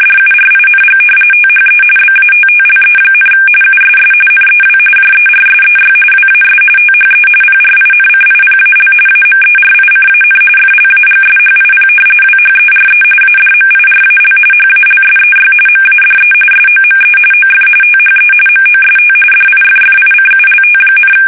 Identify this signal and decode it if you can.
RTTY, about 45.45 baud. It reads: DURING THE BATTLE, USS NEW JERSEY WAS THE PRINCIPAL AMERICAN FLAGSHIP WHERE ADMIRAL HALSEY MADE SOME OF THE MOST DIFFICULT,
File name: fsk850 / battleshipnj